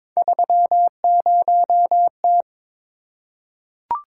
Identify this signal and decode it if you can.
Morse code, heard as a keyed tone, about 22 wpm, 700 Hz. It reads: 30T
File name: practice / 22wpm / Three Letter-Number Combinations - 22wpm